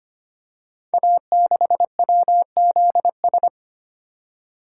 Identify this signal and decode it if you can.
Morse code, heard as a keyed tone, about 25 wpm, 700 Hz. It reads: A6WZS
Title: Code Groups - 5 Letter:Numbers - 25wpm